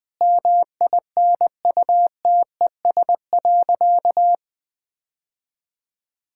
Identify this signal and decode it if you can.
Morse code, heard as a keyed tone, about 20 wpm, 700 Hz. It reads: MINUTES.
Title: Sentences with 3-Word Phrases - Spoken Beginning - 20wpm